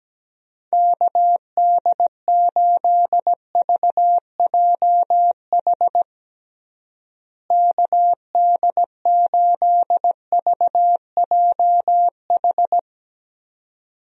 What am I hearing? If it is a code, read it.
Morse code, heard as a keyed tone, about 17 wpm, 700 Hz. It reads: KD8VJH KD8VJH
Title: USA Call Signs - 6 Characters - Repeated 2x - 17wpm